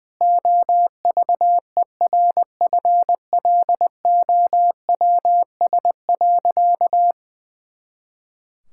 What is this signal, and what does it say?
Morse code, heard as a keyed tone, about 21 wpm, 700 Hz. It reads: OVERFLOWS.
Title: Pangram Challenge v2 - 20wpm